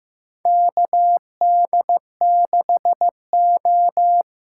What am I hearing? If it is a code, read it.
Morse code, heard as a keyed tone, about 15 wpm, 700 Hz. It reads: KD6O
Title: USA Call Signs - 6 Characters - 8x Character Spacing - 15wpm